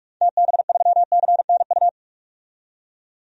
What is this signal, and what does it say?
Morse code, heard as a keyed tone, about 46 wpm, 700 Hz. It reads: TB3CNU
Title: DXCC-100 Call Signs - 6 Characters - 45wpm